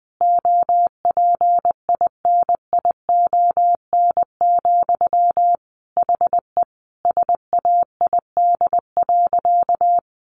Morse code, heard as a keyed tone, about 20 wpm, 700 Hz. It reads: OPINION, HE SAID.
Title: Sentences from Top 1200 Words 20wpm